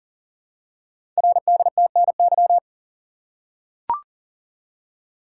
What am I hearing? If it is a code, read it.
Morse code, heard as a keyed tone, about 39 wpm, 690 Hz. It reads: RDTNY
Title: Code Groups - 5 Letters - 40wpm